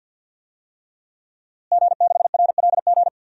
Morse code, heard as a keyed tone, about 51 wpm, 700 Hz. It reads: G6RLG
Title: DXCC-100 Call Signs - 5 Characters - Repeated 3x - 50wpm